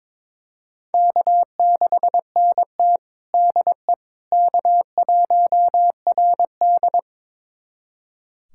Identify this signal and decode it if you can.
Morse code, heard as a keyed tone, about 21 wpm, 700 Hz. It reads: K6NT DE K1RD
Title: USA Callsign DE Callsign - 4 Characters - 22wpm